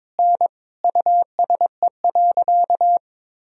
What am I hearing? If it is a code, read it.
Morse code, heard as a keyed tone, about 22 wpm, 700 Hz. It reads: N USE.
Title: Sentences from Top 100 Words and Abbreviations 22wpm